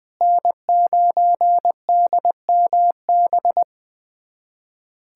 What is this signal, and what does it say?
Morse code, heard as a keyed tone, about 20 wpm, 700 Hz. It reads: N9DMB
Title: USA Callsign DE Callsign - 5 Characters - 20wpm